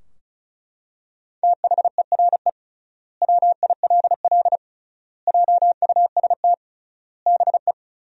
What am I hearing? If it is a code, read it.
Morse code, heard as a keyed tone, about 34 wpm, 700 Hz. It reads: THERE WILL JUST BE
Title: 3x Word-Spacing - Sentences from Top 200 Words - 35wpm